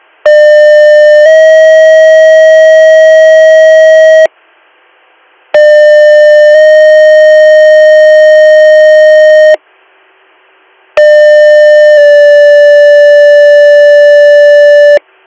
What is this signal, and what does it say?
Two-tone page, heard as a one-second tone, then a three-second tone, about 600 then 640 Hz, about 600 then 620 Hz, about 600 then 580 Hz. It is heard from 0.3 to 15.0 s.